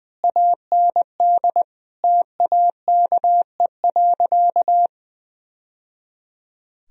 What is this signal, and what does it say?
Morse code, heard as a keyed tone, about 21 wpm, 700 Hz. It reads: AND TAKE.
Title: Sentences from Top 1000 Words - Review Part 4 - 20wpm